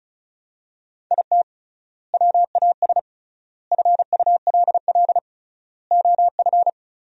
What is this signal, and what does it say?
Morse code, heard as a keyed tone, about 36 wpm, 700 Hz. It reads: IT WAS FULL OF
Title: 3x Word-Spacing - Sentences from Top 500 Words - 35wpm